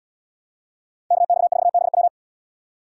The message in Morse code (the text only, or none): KX4BW